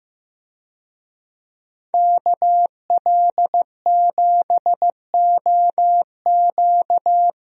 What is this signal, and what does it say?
Morse code, heard as a keyed tone, about 15 wpm, 700 Hz. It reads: KL7OQ
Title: DXCC-100 Call Signs - 5 Characters - Repeated 3x - 15wpm